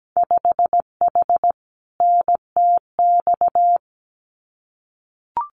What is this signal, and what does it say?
Morse code, heard as a keyed tone, about 17 wpm, 700 Hz. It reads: 5H NTX
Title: Winter Field Day Exchanges Encore 17wpm